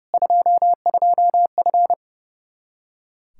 Morse code, heard as a keyed tone, about 30 wpm, 700 Hz. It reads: 22F
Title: Practice QSO - Exchange 2 - Info - 30wpm